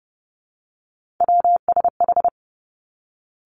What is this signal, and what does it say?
Morse code, heard as a keyed tone, about 30 wpm, 700 Hz. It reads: WSH
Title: Three Challenging Letter-Number Combinations 30wpm